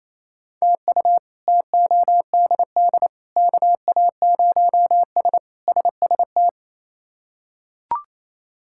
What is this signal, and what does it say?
Morse code, heard as a keyed tone, about 27 wpm, 700 Hz. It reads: TU TODD KA0S SST
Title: K1USN SST QSO Practice - 28wpm